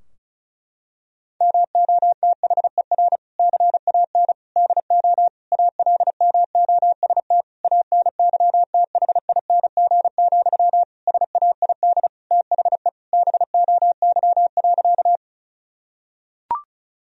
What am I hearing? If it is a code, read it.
Morse code, heard as a keyed tone, about 36 wpm, 700 Hz. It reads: MOTHER CAN DO ALMOST ANYTHING, SAID THE BOY.
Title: Sentences from Top 1000 Words - Review Part 4 - 35wpm